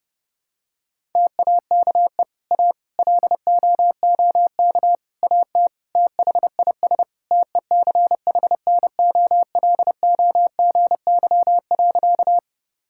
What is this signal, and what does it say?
Morse code, heard as a keyed tone, about 30 wpm, 700 Hz. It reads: TAKE A LOOK AT THIS TECHNOLOGY.
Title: Sentences from Top 1000 Words - Review Part 5 - 30wpm